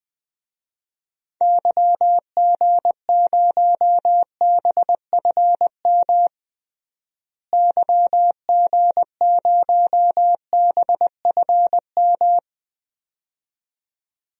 Morse code, heard as a keyed tone, about 20 wpm, 700 Hz. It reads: YG0BFM YG0BFM
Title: DXCC-100 Call Signs - 6 Characters - Repeated 2x - 20wpm